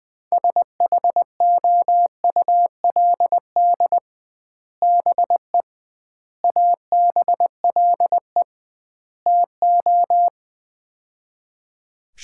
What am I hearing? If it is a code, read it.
Morse code, heard as a keyed tone, about 20 wpm, 700 Hz. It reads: SHOULD BE ABLE TO
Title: 2x Word-Spacing - Sets of 4 Words - 20wpm